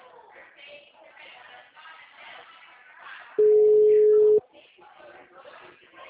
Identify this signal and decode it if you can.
Ringback tone (Europe, 425 Hz), with speech.